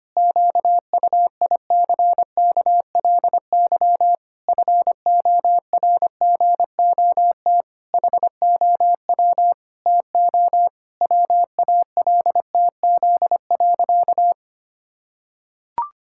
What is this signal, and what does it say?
Morse code, heard as a keyed tone, about 25 wpm, 700 Hz. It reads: QUICKLY FORGOT HOW TO WALTZ.
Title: Pangram Challenge v2 - 25wpm